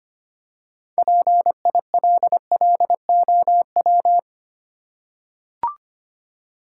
Morse code, heard as a keyed tone, about 26 wpm, 700 Hz. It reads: PILLOW
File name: practice / 25wpm / Easy Words with Hints - 25wpm